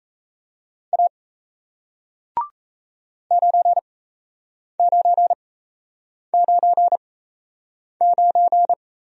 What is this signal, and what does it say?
Morse code, heard as a keyed tone, about 37 wpm, spaced as for 8 wpm, 700 Hz. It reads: A 9999